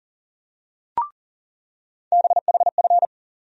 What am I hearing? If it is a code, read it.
Morse code, heard as a keyed tone, about 40 wpm, 700 Hz. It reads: BHF
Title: USA Call Sign Postfixes - 3 Letters 40wpm